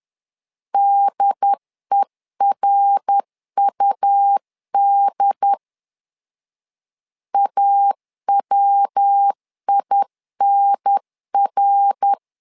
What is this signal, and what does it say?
Morse code, heard as a keyed tone, about 10 wpm, 800 Hz. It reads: DERUD AWINR
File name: LEZIONI / 3 MWRO / 3 MWR 10wpm 1952